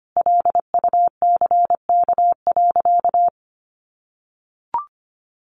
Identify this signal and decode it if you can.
Morse code, heard as a keyed tone, about 25 wpm, 700 Hz. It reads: LUCK.
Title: Sentences from Top 1500 Words 25wpm